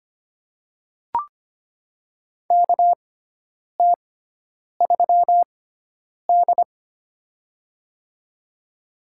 Morse code, heard as a keyed tone, about 25 wpm, spaced as for 9 wpm, 700 Hz. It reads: KT3D
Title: USA Call Signs - 6x Character Spacing - 25wpm